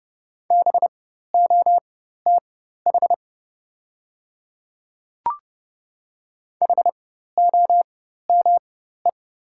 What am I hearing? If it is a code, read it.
Morse code, heard as a keyed tone, about 31 wpm, spaced as for 14 wpm, 690 Hz. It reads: BOTH HOME
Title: 4x Character Spacing - Top 500 Words - 30wpm